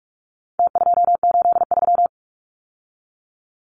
Morse code, heard as a keyed tone, about 44 wpm, 700 Hz. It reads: T283
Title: Numbers - Contest Style - 4-Digit 45wpm